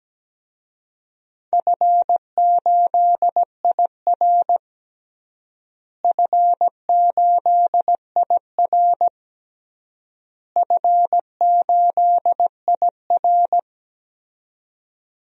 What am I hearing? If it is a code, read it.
Morse code, heard as a keyed tone, about 17 wpm, 700 Hz. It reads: F8IR F8IR F8IR
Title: DXCC-100 Call Signs - 4 Characters - Repeated 3x - 17wpm